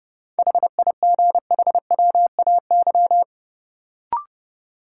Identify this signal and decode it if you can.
Morse code, heard as a keyed tone, about 30 wpm, 700 Hz. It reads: HIGHWAY
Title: Sentences from Top 1400 Words - New Word Drill - 30wpm